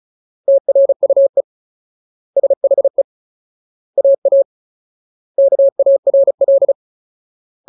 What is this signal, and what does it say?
Morse code, heard as a keyed tone, about 35 wpm, 550 Hz. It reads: TRUE SHE AA KARL
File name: Ultimate Word List / 35WPM Ultimate